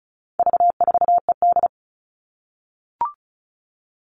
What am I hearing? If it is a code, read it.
Morse code, heard as a keyed tone, about 34 wpm, 700 Hz. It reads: V4ED
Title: All Cut Numbers - 4-Digit 35wpm